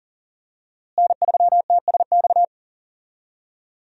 Morse code, heard as a keyed tone, about 40 wpm, 700 Hz. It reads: N3TSX
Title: USA Call Signs - 5 Characters - Repeated 2x - 40wpm